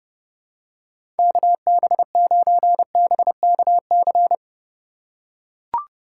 Morse code, heard as a keyed tone, about 29 wpm, 700 Hz. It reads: KB9BKC
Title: USA Call Signs - 6 Characters - 4x Character Spacing - 30wpm